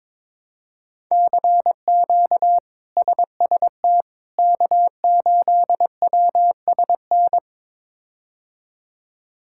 Morse code, heard as a keyed tone, about 22 wpm, 700 Hz. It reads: CQ SST K8WSN
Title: K1USN SST QSO Practice - 22wpm